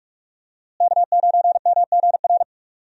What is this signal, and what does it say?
Morse code, heard as a keyed tone, about 46 wpm, 700 Hz. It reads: K9MGR
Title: USA Call Signs - 5 Characters - Repeated 3x - 45wpm